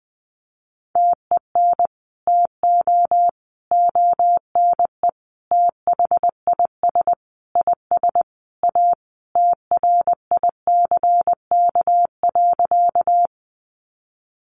Morse code, heard as a keyed tone, about 20 wpm, 700 Hz. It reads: TEN TO ONE THIS IS A TRICK.
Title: Sentences from Top 1500 Words 20wpm